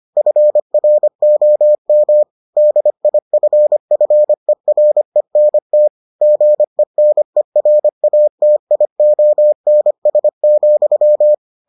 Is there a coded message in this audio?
Yes: Morse code from 0.2 to 11.3 s, heard as a keyed tone, about 25 wpm, 600 Hz. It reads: FROM DIFFERENT GENERATIONS,